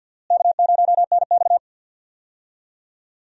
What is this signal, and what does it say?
Morse code, heard as a keyed tone, about 50 wpm, 700 Hz. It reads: K0NX
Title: USA Call Signs - 4 Characters - Repeated 2x - 50wpm